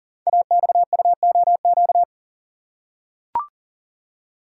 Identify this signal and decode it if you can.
Morse code, heard as a keyed tone, about 38 wpm, 690 Hz. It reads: AXUOQ
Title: Code Groups - 5 Letters - 40wpm